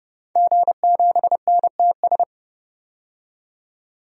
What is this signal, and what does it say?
Morse code, heard as a keyed tone, about 31 wpm, 700 Hz. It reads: G7NTS
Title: DXCC-100 Call Signs - 5 Characters - 30wpm